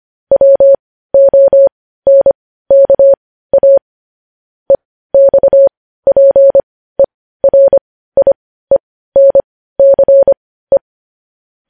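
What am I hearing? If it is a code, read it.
Morse code, heard as a keyed tone, about 25 wpm, spaced as for 15 wpm, 550 Hz. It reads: WONKA EXPERIENCE
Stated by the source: News headlines sent in Morse code at 15 words per minute, preceded by vvv.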